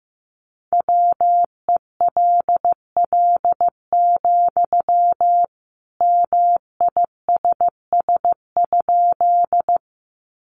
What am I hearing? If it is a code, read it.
Morse code, heard as a keyed tone, about 15 wpm, 700 Hz. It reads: WELL, MISS?
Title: Sentences from Top 500 Words 15wpm